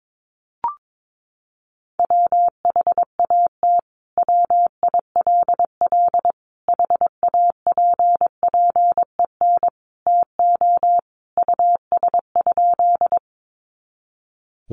Morse code, heard as a keyed tone, about 22 wpm, 700 Hz. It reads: WHAT WILL HAPPEN TO US?
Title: Sentences from Top 400 Words 22wpm